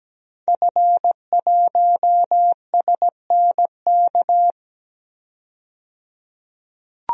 Morse code, heard as a keyed tone, about 17 wpm, 700 Hz. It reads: F1SNK
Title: DXCC-100 Call Signs - 5 Characters - 17wpm